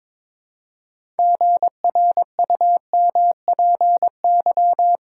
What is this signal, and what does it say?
Morse code, heard as a keyed tone, about 22 wpm, 700 Hz. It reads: GRUMPY
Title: Pangram Challenge v2 - 22wpm